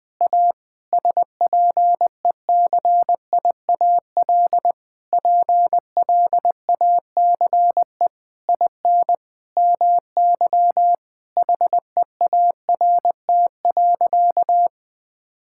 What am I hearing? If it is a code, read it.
Morse code, heard as a keyed tone, about 20 wpm, 700 Hz. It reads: A SPECIAL PLACE IN MY HEART.
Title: Sentences with 9-Word Phrases - 20wpm